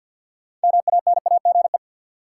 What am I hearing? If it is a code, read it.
Morse code, heard as a keyed tone, about 50 wpm, 700 Hz. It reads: MANAGE
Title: Sentences from Top 1400 Words - New Word Drill Alt - 50wpm